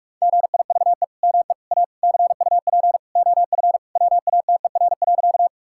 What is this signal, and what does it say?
Morse code, heard as a keyed tone, about 45 wpm, 700 Hz. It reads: GIVE ME A CUP OF WATER.
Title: Sentences from Top 1600 Words 45wpm